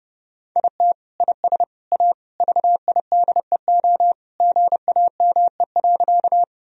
Morse code, heard as a keyed tone, about 29 wpm, 700 Hz. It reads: IT IS A VIDEO GAME.
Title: Sentences from Top 1000 Words - Review Part 4 - 30wpm